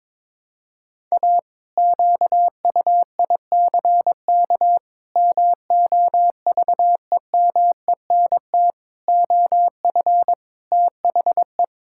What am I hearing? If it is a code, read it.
Morse code, heard as a keyed tone, about 22 wpm, 700 Hz. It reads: A QUICK MOVEMENT OF THE
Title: Pangram Challenge v2 - 22wpm